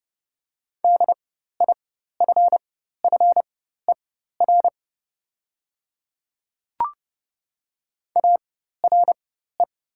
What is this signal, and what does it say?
Morse code, heard as a keyed tone, about 30 wpm, spaced as for 14 wpm, 700 Hz. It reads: DIFFER ARE
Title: 4x Character Spacing - Top 200 Words - 30wpm